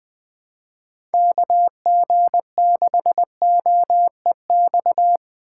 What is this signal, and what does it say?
Morse code, heard as a keyed tone, about 23 wpm, 700 Hz. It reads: KG6OEX
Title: Speed Racing - USA 2x3 Call Signs 20wpm